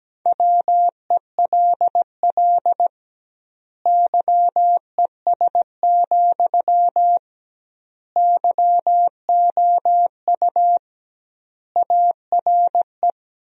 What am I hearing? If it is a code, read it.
Morse code, heard as a keyed tone, about 17 wpm, 700 Hz. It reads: WELL YES, YOU ARE
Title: 2x Word-Spacing - Sentences from Top 500 Words - 17wpm